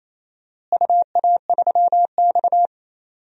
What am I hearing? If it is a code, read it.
Morse code, heard as a keyed tone, about 28 wpm, 700 Hz. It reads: UA3X
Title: DXCC-100 Call Signs - 4 Characters - Repeated 2x - 28wpm